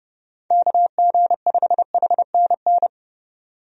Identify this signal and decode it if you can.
Morse code, heard as a keyed tone, about 29 wpm, 700 Hz. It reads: KG5HNN
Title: USA Call Signs - 6 Characters - 30wpm